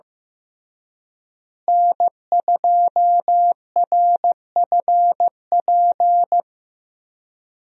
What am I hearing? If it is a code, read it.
Morse code, heard as a keyed tone, about 16 wpm, 700 Hz. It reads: N2RFP
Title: USA Call Signs - 5 Characters - Repeated 3x - 15wpm